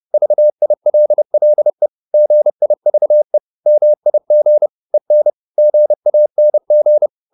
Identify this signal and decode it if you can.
Morse code, heard as a keyed tone, about 30 wpm, 600 Hz. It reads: VILLE GIVE MIG EN GANG
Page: Ferie 30 wpm